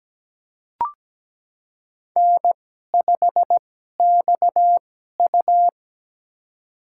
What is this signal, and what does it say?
Morse code, heard as a keyed tone, about 17 wpm, spaced as for 12 wpm, 700 Hz. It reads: N5XU
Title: USA Call Signs - 4 Characters - 2x Character Spacing - 17wpm